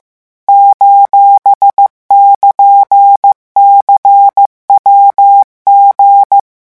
Morse code, heard as a keyed tone, about 15 wpm, 800 Hz. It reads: :<KN>CWG
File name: morse mp3 / revision17-vitesse 14 mots